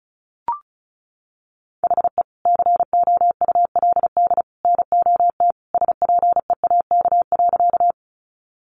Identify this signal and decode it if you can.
Morse code, heard as a keyed tone, about 34 wpm, 700 Hz. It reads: HE COULD NOT SPEAK.
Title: Sentences from Top 1100 Words 35wpm